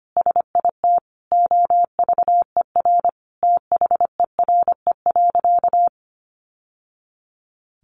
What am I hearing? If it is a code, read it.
Morse code, heard as a keyed tone, about 25 wpm, 700 Hz. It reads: SIT OVER THERE.
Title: Sentences from Top 1100 Words 25wpm